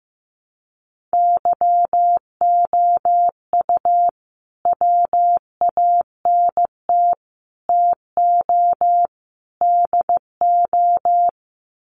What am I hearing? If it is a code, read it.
Morse code, heard as a keyed tone, about 15 wpm, 700 Hz. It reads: YOU WANT TO DO
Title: Sets of 4 Words 15wpm